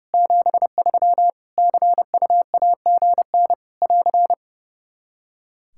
Morse code, heard as a keyed tone, about 30 wpm, 700 Hz. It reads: 73 CUAGN <AR>
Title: K1USN SST QSO Practice 30wpm